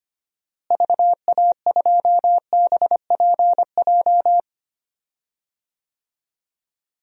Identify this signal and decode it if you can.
Morse code, heard as a keyed tone, about 25 wpm, 700 Hz. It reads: VA2BPJ
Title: DXCC-100 Call Signs - 6 Characters - Repeated 3x - 25wpm